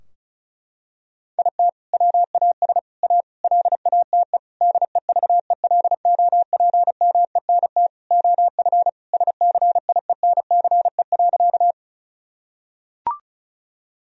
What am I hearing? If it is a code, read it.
Morse code, heard as a keyed tone, about 36 wpm, 700 Hz. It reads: IT WAS A LATE DEVELOPMENT OF SCIENCE.
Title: Sentences from Top 1000 Words - Review Part 4 - 35wpm